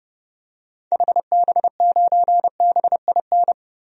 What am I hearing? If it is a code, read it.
Morse code, heard as a keyed tone, about 30 wpm, 700 Hz. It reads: HB9BIN
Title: DXCC-100 Call Signs - 6 Characters - 30wpm